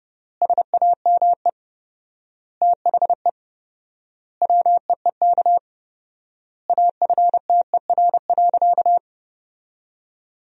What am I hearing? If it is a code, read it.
Morse code, heard as a keyed tone, about 30 wpm, 700 Hz. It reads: SAME THE WEEK AFTER.
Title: 4x Word-Spacing - Sentences from Top 500 Words - 30wpm